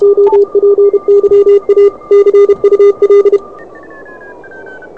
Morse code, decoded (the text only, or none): OPYA CUL